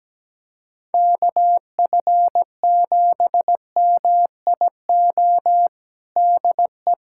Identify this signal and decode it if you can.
Morse code, heard as a keyed tone, about 17 wpm, 700 Hz. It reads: KF7MIO DE
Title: USA Callsign DE Callsign 17wpm